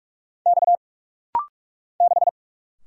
Morse code, heard as a keyed tone, about 48 wpm, 700 Hz. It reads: X 6
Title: Single Letter-Number - Rapid-Fire 45wpm